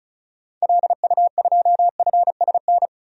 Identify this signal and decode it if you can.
Morse code, heard as a keyed tone, about 36 wpm, 700 Hz. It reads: LU2FSN
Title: DXCC-100 Call Signs - 6 Characters - 35wpm